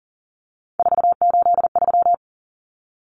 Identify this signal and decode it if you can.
Morse code, heard as a keyed tone, about 40 wpm, 700 Hz. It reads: 483
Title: Numbers - 3-Digit 40wpm